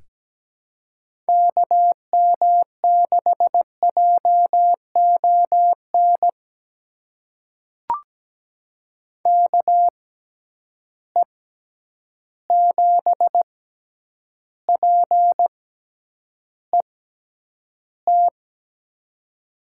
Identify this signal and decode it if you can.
Morse code, heard as a keyed tone, about 17 wpm, spaced as for 9 wpm, 700 Hz. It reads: KM6JON KE7PET
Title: USA Call Signs - 6 Characters - 6x Character Spacing - 17wpm